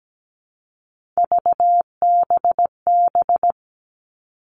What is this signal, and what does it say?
Morse code, heard as a keyed tone, about 17 wpm, 700 Hz. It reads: VBB
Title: Three Challenging Letter-Number Combinations 17wpm